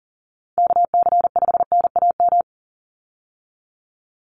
Morse code, heard as a keyed tone, about 38 wpm, 690 Hz. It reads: KC5NAM
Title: USA Call Signs 40wpm